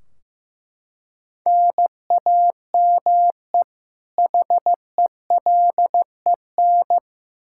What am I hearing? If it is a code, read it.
Morse code, heard as a keyed tone, about 15 wpm, 700 Hz. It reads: NAME HELEN
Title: Practice QSO - Exchange 1 - Beginning - 15wpm